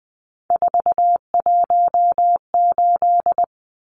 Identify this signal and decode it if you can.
Morse code, heard as a keyed tone, about 20 wpm, 700 Hz. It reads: 418
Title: Numbers - 3-Digit 20wpm